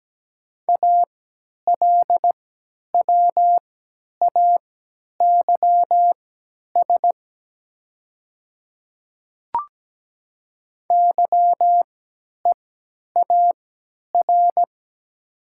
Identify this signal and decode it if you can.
Morse code, heard as a keyed tone, about 17 wpm, spaced as for 10 wpm, 700 Hz. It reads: ALWAYS YEAR
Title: 3x Character Spacing - Top 300 Words - 17wpm